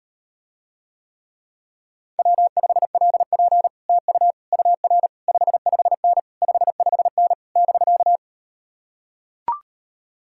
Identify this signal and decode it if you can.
Morse code, heard as a keyed tone, about 38 wpm, 690 Hz. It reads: W5LP TU UR 55N 55N <BK>